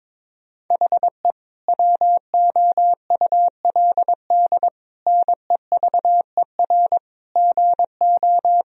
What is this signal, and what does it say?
Morse code, heard as a keyed tone, about 22 wpm, 700 Hz. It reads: HE WOULD NEVER GO
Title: Sentences from Top 1000 Words - Review Part 2 - 22wpm